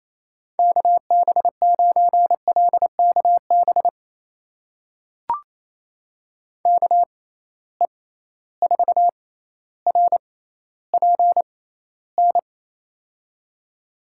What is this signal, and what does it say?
Morse code, heard as a keyed tone, about 28 wpm, spaced as for 14 wpm, 700 Hz. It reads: KB9LKB KE4RPN
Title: USA Call Signs - 6 Characters - 6x Character Spacing - 28wpm